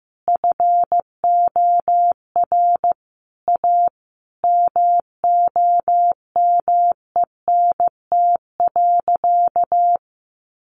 Morse code, heard as a keyed tone, about 15 wpm, 700 Hz. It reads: FOR A MOMENT.
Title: Sentences from Top 1500 Words 15wpm